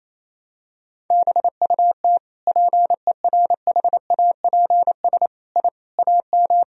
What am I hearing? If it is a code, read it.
Morse code, heard as a keyed tone, about 28 wpm, 700 Hz. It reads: BUT PERHAPS I AM
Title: Sentences from Top 1000 Words - Review Part 2 - 28wpm